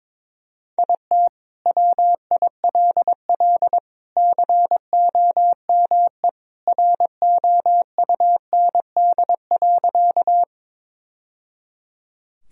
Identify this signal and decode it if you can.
Morse code, heard as a keyed tone, about 22 wpm, 700 Hz. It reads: IT WILL COME ROUND.
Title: Sentences from Top 1000 Words - Review Part 1 - 22wpm